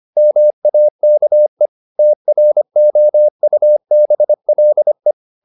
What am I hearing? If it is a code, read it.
Morse code, heard as a keyed tone, about 25 wpm, 600 Hz. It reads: MAKE TROUBLE
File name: the old man and the sea part 1 25wpm